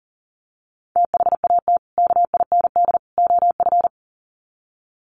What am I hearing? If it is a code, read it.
Morse code, heard as a keyed tone, about 38 wpm, 700 Hz. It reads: THAT KIND OF
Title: Sets of 3 Words 40wpm